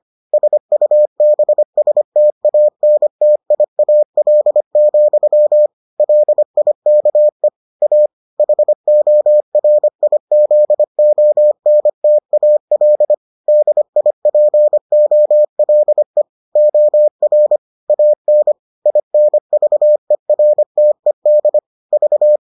Morse code, heard as a keyed tone, about 25 wpm, 600 Hz. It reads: SUBSTANTIAL, LIKE A HORIZONTAL DIPOLE OR AN INVERTED V